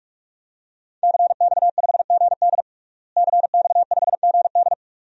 Morse code, heard as a keyed tone, about 42 wpm, 700 Hz. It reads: CX5GD CX5GD
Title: DXCC-100 Call Signs - 5 Characters - Repeated 2x - 45wpm